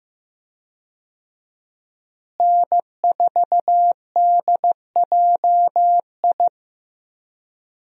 Morse code, heard as a keyed tone, about 15 wpm, 700 Hz. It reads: N4DJI
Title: USA Call Signs - 5 Characters - Repeated 3x - 15wpm